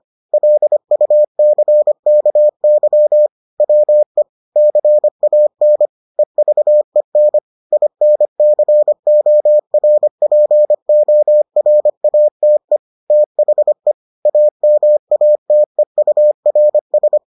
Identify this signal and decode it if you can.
Morse code, heard as a keyed tone, about 25 wpm, 600 Hz. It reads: LUCKY WE CAN EVEN INCORPORATE THE AMATEURS